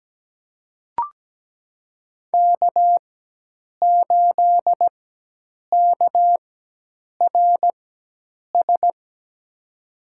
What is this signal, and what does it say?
Morse code, heard as a keyed tone, about 17 wpm, spaced as for 8 wpm, 700 Hz. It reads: K8KRS
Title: USA Call Signs - 5 Characters - 4x Character Spacing - 17wpm